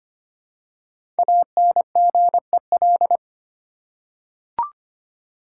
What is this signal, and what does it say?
Morse code, heard as a keyed tone, about 25 wpm, 700 Hz. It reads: ANGEL
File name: 5 Letter Words - First 3 Letters Given - 25wpm